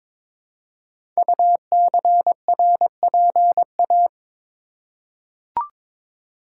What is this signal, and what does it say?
Morse code, heard as a keyed tone, about 22 wpm, 700 Hz. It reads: UCRPA
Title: Code Groups - 5 Letter:Numbers - 22wpm